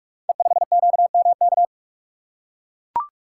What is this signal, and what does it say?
Morse code, heard as a keyed tone, about 44 wpm, 690 Hz. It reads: E5QMK
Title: Code Groups - 5 Letter:Numbers - 45wpm